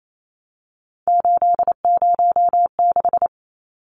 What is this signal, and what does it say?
Morse code, heard as a keyed tone, about 28 wpm, 700 Hz. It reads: 806
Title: Numbers - 3-Digit 28wpm